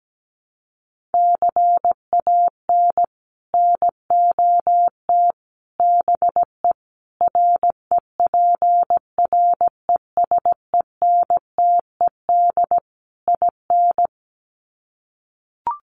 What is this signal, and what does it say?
Morse code, heard as a keyed tone, about 17 wpm, 700 Hz. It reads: CAN NOT BE REPRESENTED IN
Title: Sets of 5 Words 17wpm